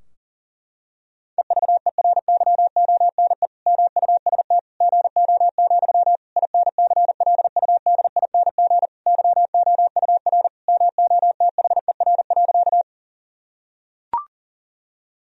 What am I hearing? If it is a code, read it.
Morse code, heard as a keyed tone, about 40 wpm, 700 Hz. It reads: EVERYONE MUST GO, INCLUDING YOUR MOTHER.
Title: Sentences from Top 1000 Words - Review Part 4 - 40wpm